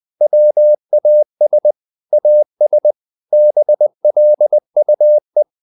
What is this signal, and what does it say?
Morse code, heard as a keyed tone, about 20 wpm, 600 Hz. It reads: WAS AS BLUE
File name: the old man and the sea part 2 20wpm